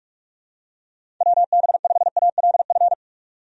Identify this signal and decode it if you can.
Morse code, heard as a keyed tone, about 46 wpm, 700 Hz. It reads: WB5ALF
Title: USA Call Signs - 6 Characters - 45wpm